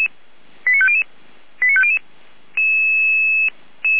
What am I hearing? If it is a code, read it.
Five-tone selective call (ZVEI), heard as quick runs of five tones, tones of about 70 ms. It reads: 89500 89500